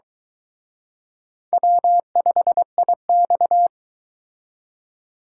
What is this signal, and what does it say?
Morse code, heard as a keyed tone, about 22 wpm, 700 Hz. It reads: W5IX